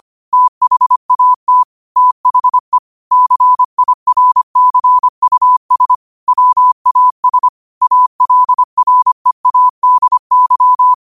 Morse code, heard as a keyed tone, about 25 wpm, 1000 Hz. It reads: THAT THE CIRCUS WAS ALREADY
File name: qotdfiles / 25wpm